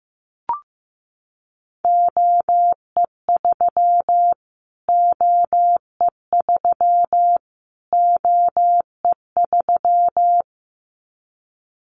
Morse code, heard as a keyed tone, about 15 wpm, 700 Hz. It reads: OE3 OE3 OE3
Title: Callsign Prefixes for Top 25 DXCC - Repeated 3x - 15wpm